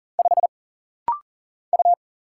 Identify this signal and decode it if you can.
Morse code, heard as a keyed tone, about 40 wpm, 700 Hz. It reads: 5 U
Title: Single Letter-Number Alt - Rapid-Fire 40wpm